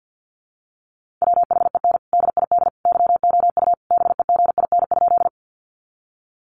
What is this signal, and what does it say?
Morse code, heard as a keyed tone, about 50 wpm, 700 Hz. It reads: WHEN DID YOU BEGIN?
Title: Sentences from Top 300 Words 50wpm